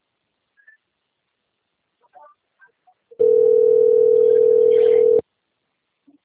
Ringback tone (North America), 440 + 480 Hz.